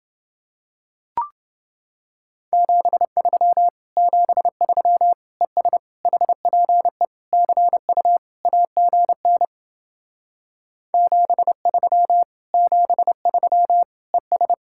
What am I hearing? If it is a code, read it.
Morse code, heard as a keyed tone, about 30 wpm, 700 Hz. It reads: 73 73 ES HPE CU AGN 73 73 ES